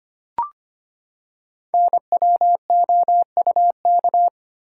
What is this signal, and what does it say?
Morse code, heard as a keyed tone, about 25 wpm, 700 Hz. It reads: NWOUK
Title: Code Groups - 5 Letters - 25wpm